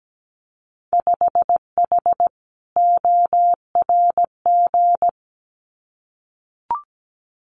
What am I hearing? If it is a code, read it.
Morse code, heard as a keyed tone, about 17 wpm, 700 Hz. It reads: 5H ORG
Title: Winter Field Day Exchanges Encore 17wpm